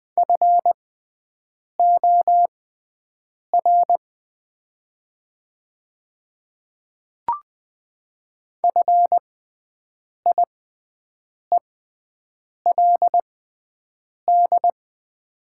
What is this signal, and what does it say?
Morse code, heard as a keyed tone, about 20 wpm, spaced as for 7 wpm, 700 Hz. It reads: FOR FIELD
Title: 6x Character Spacing - Top 500 Words - 20wpm